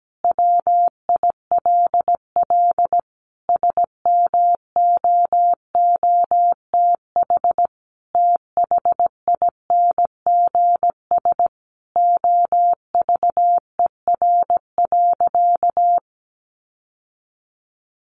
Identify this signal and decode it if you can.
Morse code, heard as a keyed tone, about 17 wpm, 700 Hz. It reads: WILL SMOOTH THINGS OVER.
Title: Sentences from Top 1400 Words 17wpm